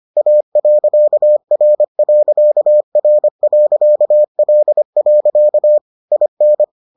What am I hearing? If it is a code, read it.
Morse code, heard as a keyed tone, about 25 wpm, 600 Hz. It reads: A.R.R.L. IN